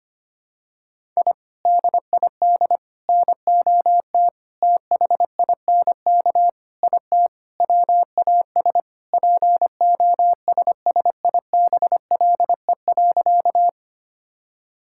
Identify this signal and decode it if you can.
Morse code, heard as a keyed tone, about 25 wpm, 700 Hz. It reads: I DID NOT THINK IT WAS POSSIBLE.
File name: practice / 25wpm / Sentences with 4-Word Phrases - 25wpm